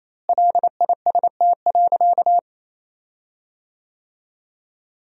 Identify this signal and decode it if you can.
Morse code, heard as a keyed tone, about 28 wpm, 700 Hz. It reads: LIST.
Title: 4x Word-Spacing - Sentences from Top 400 Words - 28wpm